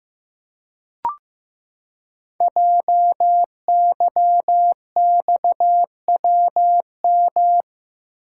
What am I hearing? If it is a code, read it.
Morse code, heard as a keyed tone, about 15 wpm, 700 Hz. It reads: JYXWM
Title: Code Groups - 5 Letters - 15wpm